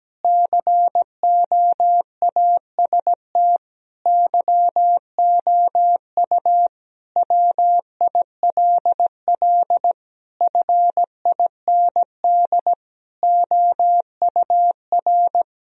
Morse code, heard as a keyed tone, about 17 wpm, 700 Hz. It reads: COAST YOU WILL FIND OUR
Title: Sentences from Top 1000 Words - Review Part 4 - 17wpm